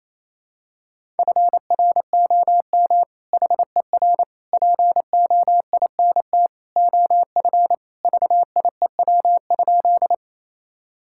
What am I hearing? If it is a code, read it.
Morse code, heard as a keyed tone, about 28 wpm, 700 Hz. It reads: FROM HER POINT OF VIEW?
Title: Sentences from Top 1000 Words - Review Part 6 - 28wpm